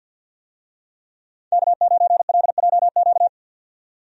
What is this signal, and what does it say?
Morse code, heard as a keyed tone, about 50 wpm, 700 Hz. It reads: K9LJQ
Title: USA Call Signs - 5 Characters - 50wpm